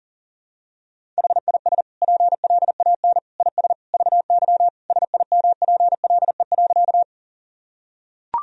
Morse code, heard as a keyed tone, about 40 wpm, 700 Hz. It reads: HIS PLAN IS VY SIMPLE.
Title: Sentences from Top 500 Words and Abbreviations 40wpm